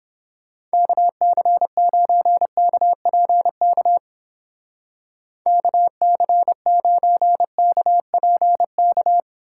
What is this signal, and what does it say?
Morse code, heard as a keyed tone, about 29 wpm, 700 Hz. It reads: KC9KPK KC9KPK